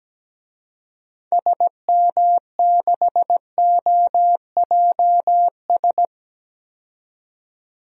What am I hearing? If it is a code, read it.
Morse code, heard as a keyed tone, about 17 wpm, 700 Hz. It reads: SM6OJS
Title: DXCC-100 Call Signs - 6 Characters - 17wpm